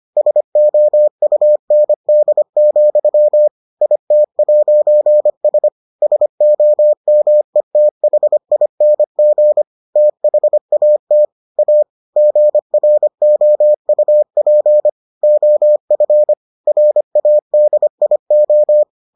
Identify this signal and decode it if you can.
Morse code, heard as a keyed tone, about 25 wpm, 600 Hz. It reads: SOUND, IT'S SOMETHING THAT A GROUP OF RADIO